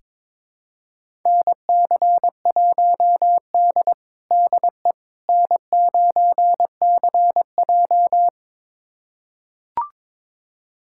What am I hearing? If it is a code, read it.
Morse code, heard as a keyed tone, about 22 wpm, 700 Hz. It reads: NC1D DE N9CJ
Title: USA Callsign DE Callsign - 4 Characters - 22wpm